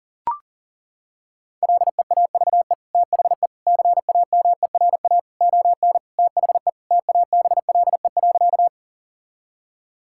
Morse code, heard as a keyed tone, about 40 wpm, 700 Hz. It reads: LEAVE THE CAMERA ON THE TABLE.
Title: Sentences from Top 1000 Words - Review Part 5 - 40wpm